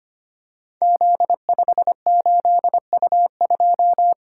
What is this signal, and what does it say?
Morse code, heard as a keyed tone, about 25 wpm, 700 Hz. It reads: Z58U2
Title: Code Groups - 5 Letter:Numbers - 25wpm